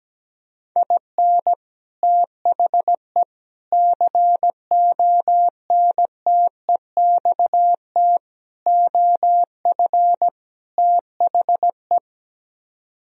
Morse code, heard as a keyed tone, about 17 wpm, 700 Hz. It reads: IN THE CONTEXT OF THE
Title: Speed-Racing - Sets of 5 Words 17wpm